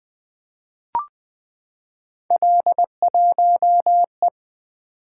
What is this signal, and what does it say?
Morse code, heard as a keyed tone, about 20 wpm, 700 Hz. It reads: L1E
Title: Three Letter-Number Combinations - 20wpm